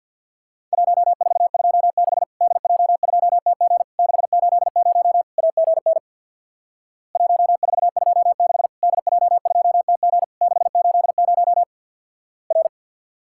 Morse code, heard as a keyed tone, about 50 wpm, 700 Hz. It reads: 1426 DJ2TG 680 AGN 1426 DJ2TG 680 R
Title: Worked All Europe DX Contest - QTC Exchange with Repeats - 50wpm